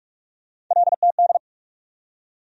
Morse code, heard as a keyed tone, about 46 wpm, 700 Hz. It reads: LTD
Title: USA Call Sign Postfixes - 3 Letters 45wpm